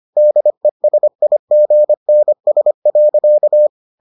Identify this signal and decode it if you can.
Morse code, heard as a keyed tone, about 25 wpm, 600 Hz. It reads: DESIGNS.